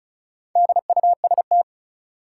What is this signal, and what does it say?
Morse code, heard as a keyed tone, about 34 wpm, 700 Hz. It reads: DUST
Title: Four Letter Words - 35wpm